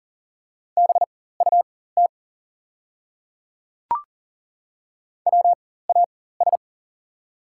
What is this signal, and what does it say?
Morse code, heard as a keyed tone, about 40 wpm, spaced as for 19 wpm, 690 Hz. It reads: BUT WAS
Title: 4x Character Spacing - Top 100 Words - 40wpm